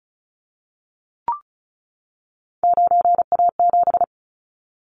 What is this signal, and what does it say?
Morse code, heard as a keyed tone, about 36 wpm, 700 Hz. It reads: 9A7
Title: Callsign Prefixes for Top 50 DXCC - 35wpm